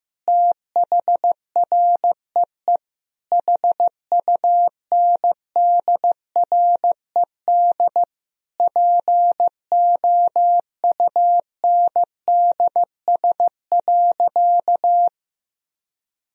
Morse code, heard as a keyed tone, about 16 wpm, 700 Hz. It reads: THREE HUNDRED POUNDS.
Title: Sentences from Top 1000 Words - Review Part 4 - 15wpm